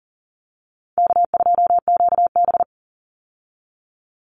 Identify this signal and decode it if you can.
Morse code, heard as a keyed tone, about 40 wpm, 700 Hz. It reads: K2QB
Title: Easy Call Signs - 40wpm